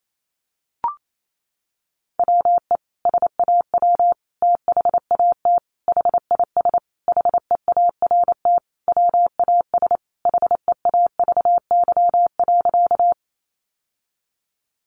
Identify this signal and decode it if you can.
Morse code, heard as a keyed tone, about 28 wpm, 700 Hz. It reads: WE SAW THAT HIS HEART WAS HEAVY.
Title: Sentences from Top 1200 Words 28wpm